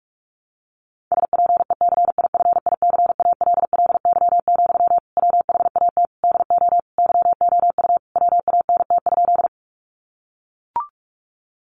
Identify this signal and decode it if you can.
Morse code, heard as a keyed tone, about 44 wpm, 700 Hz. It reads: SPECIFICALLY, WHAT DO YOU WANT?
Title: Sentences from Top 1100 Words 45wpm